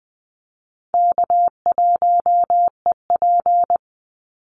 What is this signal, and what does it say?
Morse code, heard as a keyed tone, about 20 wpm, 700 Hz. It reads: K1EP
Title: Call Signs Repeated 3x - 20wpm